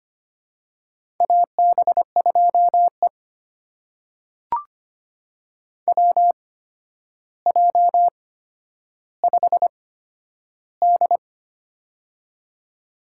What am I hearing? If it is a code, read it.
Morse code, heard as a keyed tone, about 25 wpm, spaced as for 11 wpm, 700 Hz. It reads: AB2E WJ5D
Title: USA Call Signs - 4 Characters - 8x Character Spacing - 25wpm